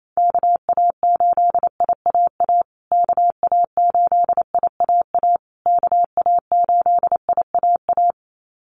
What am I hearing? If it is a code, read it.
Morse code, heard as a keyed tone, about 28 wpm, 700 Hz. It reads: KA8IAA KA8IAA KA8IAA
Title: USA Call Signs Repeated 3x 28wpm